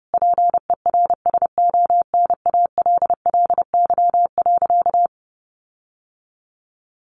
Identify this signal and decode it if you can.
Morse code, heard as a keyed tone, about 30 wpm, 700 Hz. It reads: PERSONALLY.
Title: Sentences from Top 1400 Words 30wpm